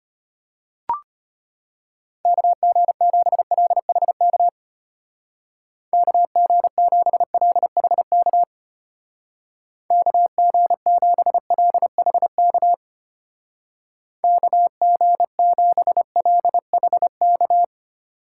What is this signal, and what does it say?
Morse code, heard as a keyed tone, about 32 wpm, 700 Hz. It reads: KG7LHK KG7LHK KG7LHK KG7LHK